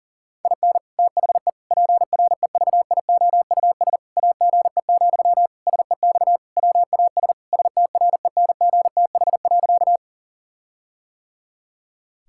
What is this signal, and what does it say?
Morse code, heard as a keyed tone, about 37 wpm, 700 Hz. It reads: IN THE PREVIOUS AGE, SEX WAS STRENGTH.
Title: Sentences from Top 1600 Words 40wpm